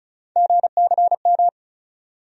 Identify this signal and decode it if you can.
Morse code, heard as a keyed tone, about 36 wpm, 700 Hz. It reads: GCM
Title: USA Call Sign Postfixes - 3 Letters 35wpm